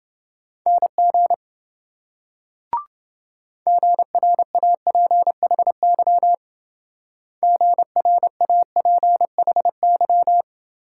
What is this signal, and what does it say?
Morse code, heard as a keyed tone, about 25 wpm, 700 Hz. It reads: NG GRAPHY GRAPHY
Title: Speed-Racing Multi-letter Phonemes - 20wpm